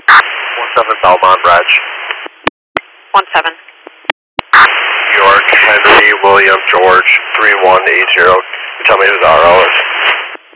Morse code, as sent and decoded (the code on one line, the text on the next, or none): none